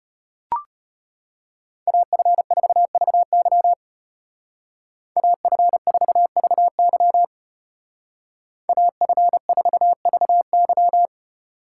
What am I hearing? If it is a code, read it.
Morse code, heard as a keyed tone, about 32 wpm, 700 Hz. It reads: AF4VY AF4VY AF4VY